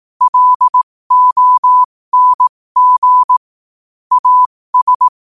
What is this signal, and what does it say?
Morse code, heard as a keyed tone, about 18 wpm, spaced as for 15 wpm, 1000 Hz. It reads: LONG AS
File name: qotdfiles / 15wpm